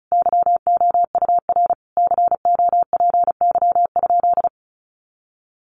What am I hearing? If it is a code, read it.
Morse code, heard as a keyed tone, about 34 wpm, 700 Hz. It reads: YOUR COPY?
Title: Sentences from Top 1400 Words 35wpm